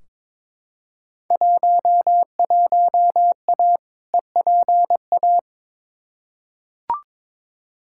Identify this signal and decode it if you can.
Morse code, heard as a keyed tone, about 22 wpm, 700 Hz. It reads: 11A EPA
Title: Field Day Exchanges Encore - 22wpm